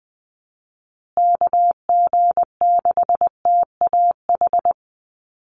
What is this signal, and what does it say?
Morse code, heard as a keyed tone, about 20 wpm, 700 Hz. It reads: KG6TAH
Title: USA Call Signs Repeated 3x 20wpm